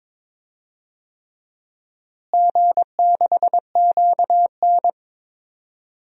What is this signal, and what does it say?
Morse code, heard as a keyed tone, about 22 wpm, 700 Hz. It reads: G6QN
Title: DXCC-100 Call Signs - 4 Characters - Repeated 3x - 22wpm